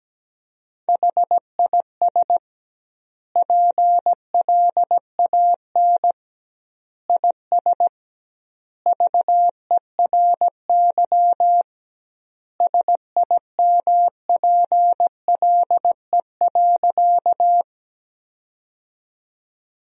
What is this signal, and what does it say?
Morse code, heard as a keyed tone, about 17 wpm, 700 Hz. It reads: HIS PLAN IS VERY SIMPLE.
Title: 2x Word-Spacing - Sentences from Top 500 Words - 17wpm